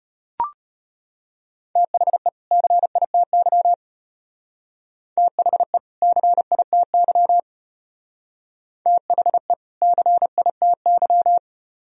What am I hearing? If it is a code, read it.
Morse code, heard as a keyed tone, about 33 wpm, 700 Hz. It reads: THE CITY THE CITY THE CITY